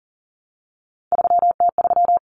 Morse code, heard as a keyed tone, about 40 wpm, 700 Hz. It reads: 3T3
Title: Common Cut Numbers - 3-Digit 40wpm